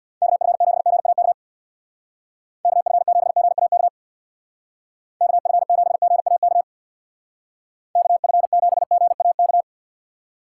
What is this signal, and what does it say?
Morse code, heard as a keyed tone, about 61 wpm, 700 Hz. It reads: KF7GAK KF7GAK KF7GAK KF7GAK
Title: Speed Racing - USA 2x3 Call Signs 50wpm